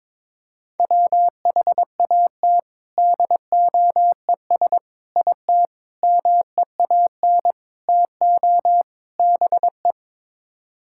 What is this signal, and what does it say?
Morse code, heard as a keyed tone, about 22 wpm, 700 Hz. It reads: WHAT DOES IT MEAN TO BE
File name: practice / 22wpm / Sets of 6 Words - 22wpm